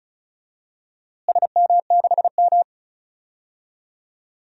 Morse code, heard as a keyed tone, about 34 wpm, 700 Hz. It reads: SM6M
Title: DXCC-100 Call Signs - 4 Characters - Repeated 2x - 35wpm